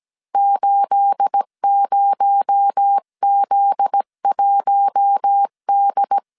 Morse code, heard as a keyed tone, about 17 wpm, 800 Hz. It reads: 80Z1D
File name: CW / DET / 1 9 1500C S B 16wpm 2856